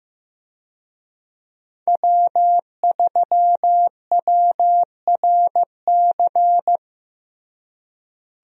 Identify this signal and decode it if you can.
Morse code, heard as a keyed tone, about 15 wpm, 700 Hz. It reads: W3WRC
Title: USA Call Signs - 5 Characters - Repeated 2x - 15wpm